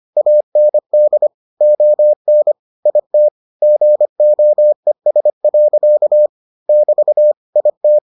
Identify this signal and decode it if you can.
Morse code, heard as a keyed tone, about 25 wpm, 600 Hz. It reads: AND ON IT GOES. <BT> IT